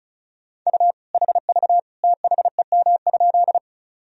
Morse code, heard as a keyed tone, about 34 wpm, 700 Hz. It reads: U HV THEM?
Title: Sentences from Top 100 Words and Abbreviations 35wpm